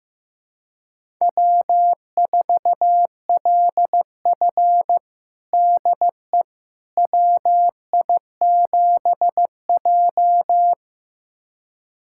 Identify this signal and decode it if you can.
Morse code, heard as a keyed tone, about 15 wpm, 700 Hz. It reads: W4LF DE WI7J
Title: USA Callsign DE Callsign - 4 Characters - 15wpm